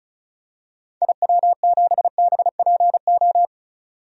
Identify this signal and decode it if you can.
Morse code, heard as a keyed tone, about 34 wpm, 700 Hz. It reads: IW7BPO
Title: DXCC-100 Call Signs - 6 Characters - 35wpm